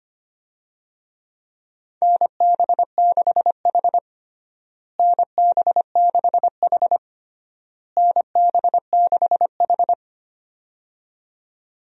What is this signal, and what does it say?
Morse code, heard as a keyed tone, about 25 wpm, 700 Hz. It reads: NB6H NB6H NB6H
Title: USA Call Signs - 4 Characters - Repeated 3x - 25wpm